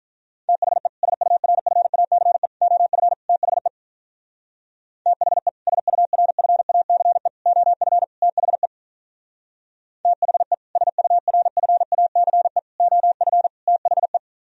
Morse code, heard as a keyed tone, about 42 wpm, 700 Hz. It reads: THE SURFACE OF THE THE SURFACE OF THE THE SURFACE OF THE
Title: Speed-Racing - Sets of 4 Words 35wpm